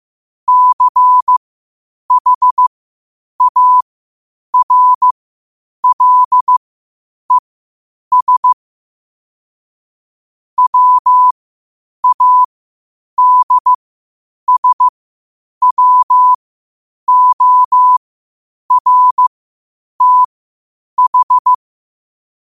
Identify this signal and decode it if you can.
Morse code, heard as a keyed tone, about 15 wpm, spaced as for 8 wpm, 1000 Hz. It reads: CHARLES WADSWORTH
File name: qotdfiles / 8wpm